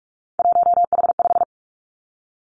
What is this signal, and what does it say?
Morse code, heard as a keyed tone, about 46 wpm, 700 Hz. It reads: 1H5
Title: Three Challenging Letter-Number Combinations 45wpm